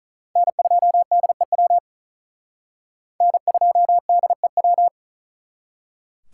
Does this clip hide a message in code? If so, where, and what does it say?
Yes: Morse code from 0.4 to 4.9 s, heard as a keyed tone, about 45 wpm, 700 Hz. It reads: N2DEW N2DEW